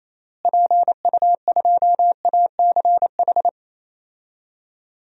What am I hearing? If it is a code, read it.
Morse code, heard as a keyed tone, about 28 wpm, 700 Hz. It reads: PU2ACH
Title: DXCC-100 Call Signs - 6 Characters - Repeated 3x - 28wpm